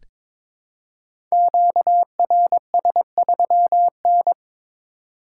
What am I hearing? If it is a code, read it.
Morse code, heard as a keyed tone, about 22 wpm, 700 Hz. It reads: QRS3N
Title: Code Groups - 5 Letter:Numbers - 22wpm